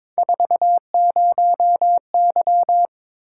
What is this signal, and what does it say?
Morse code, heard as a keyed tone, about 22 wpm, 700 Hz. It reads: 40Y
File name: practice / 22wpm / Three Letter-Number Combinations - 22wpm